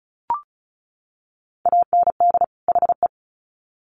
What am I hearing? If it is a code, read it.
Morse code, heard as a keyed tone, about 34 wpm, 700 Hz. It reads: AND HE
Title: Sets of 2 Words 35wpm